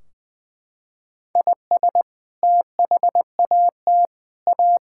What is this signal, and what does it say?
Morse code, heard as a keyed tone, about 20 wpm, 700 Hz. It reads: IS THAT A
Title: Sentences from Top 1600 Words 20wpm